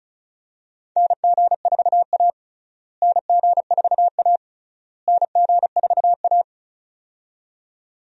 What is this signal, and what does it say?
Morse code, heard as a keyed tone, about 36 wpm, 700 Hz. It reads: NG4A NG4A NG4A
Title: USA Call Signs - 4 Characters - Repeated 3x - 35wpm